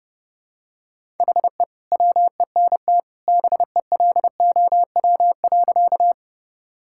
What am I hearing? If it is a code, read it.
Morse code, heard as a keyed tone, about 30 wpm, 700 Hz. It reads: HE WENT BELOW.
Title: Sentences from Top 1000 Words - Review Part 5 - 30wpm